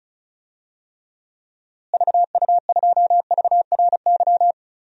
Morse code, heard as a keyed tone, about 34 wpm, 700 Hz. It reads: VU2VRY
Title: DXCC-100 Call Signs - 6 Characters - Repeated 2x - 35wpm